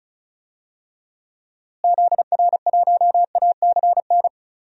Morse code, heard as a keyed tone, about 36 wpm, 700 Hz. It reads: ZR1ACN
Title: DXCC-100 Call Signs - 6 Characters - Repeated 3x - 35wpm